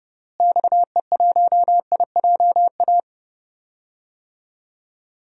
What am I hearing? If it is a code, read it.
Morse code, heard as a keyed tone, about 30 wpm, 700 Hz. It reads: XE1IJA
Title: DXCC-100 Call Signs - 6 Characters - Repeated 2x - 30wpm